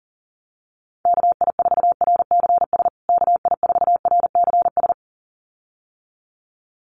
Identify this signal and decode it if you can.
Morse code, heard as a keyed tone, about 40 wpm, 700 Hz. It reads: KI4RCS KI4RCS
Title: USA Call Signs Repeated 2x 40wpm